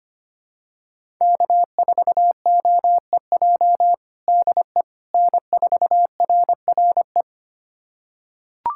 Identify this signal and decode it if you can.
Morse code, heard as a keyed tone, about 25 wpm, 700 Hz. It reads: K4OEJ DE N4RRE
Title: USA Callsign DE Callsign - 5 Characters - 25wpm